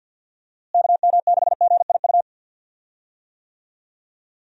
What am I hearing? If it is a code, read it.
Morse code, heard as a keyed tone, about 47 wpm, 700 Hz. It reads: KM6GIU
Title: USA Call Signs - 6 Characters - Repeated 2x - 50wpm